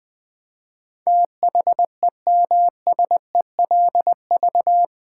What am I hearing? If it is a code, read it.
Morse code, heard as a keyed tone, about 20 wpm, 700 Hz. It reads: THEMSELV
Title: Build A Word - 4 Letters At A Time - 10-Letter Words - 20wpm